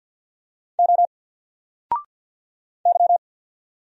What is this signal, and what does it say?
Morse code, heard as a keyed tone, about 48 wpm, 700 Hz. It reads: O Y
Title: Single Letter-Number 50wpm